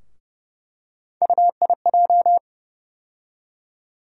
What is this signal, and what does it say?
Morse code, heard as a keyed tone, about 30 wpm, 700 Hz. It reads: UIJ
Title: USA Call Sign Postfixes - 3 Letters 30wpm